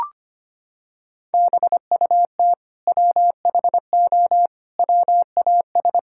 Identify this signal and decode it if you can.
Morse code, heard as a keyed tone, about 25 wpm, 700 Hz. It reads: BUT WHO WAS
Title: Sentences from Top 1000 Words - Review Part 5 - 25wpm